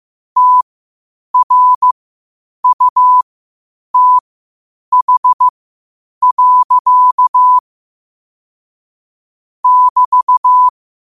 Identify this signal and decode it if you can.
Morse code, heard as a keyed tone, about 15 wpm, spaced as for 8 wpm, 1000 Hz. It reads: TRUTH. <BT>
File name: qotdfiles / 8wpm